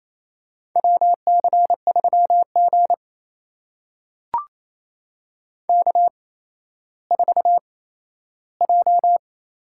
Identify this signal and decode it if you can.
Morse code, heard as a keyed tone, about 28 wpm, spaced as for 14 wpm, 700 Hz. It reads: WC3G K4J
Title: USA Call Signs - 8x Character Spacing - 28wpm